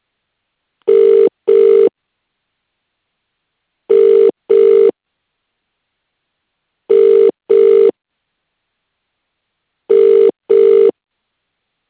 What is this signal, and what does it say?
Ringback tone (UK), 400 + 450 Hz.